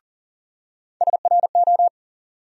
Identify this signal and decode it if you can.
Morse code, heard as a keyed tone, about 40 wpm, 700 Hz. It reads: SRO
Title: USA Call Sign Postfixes - 3 Letters 40wpm